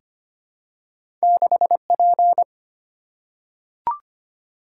Morse code, heard as a keyed tone, about 25 wpm, 700 Hz. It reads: 6P
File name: Two Letter-Number Combinations - 25wpm